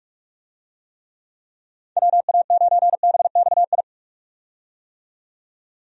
Morse code, heard as a keyed tone, about 46 wpm, 700 Hz. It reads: WA9BXI
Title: USA Call Signs - 6 Characters - 45wpm